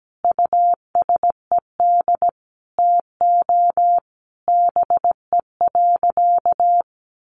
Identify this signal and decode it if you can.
Morse code, heard as a keyed tone, about 17 wpm, 700 Hz. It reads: USED TO BE.
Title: Sentences from Top 1100 Words 17wpm